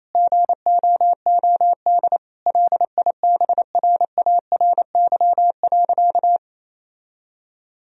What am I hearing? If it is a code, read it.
Morse code, heard as a keyed tone, about 27 wpm, 700 Hz. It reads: GOOD LIBRARY.
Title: Sentences from Top 1000 Words - Review Part 4 - 28wpm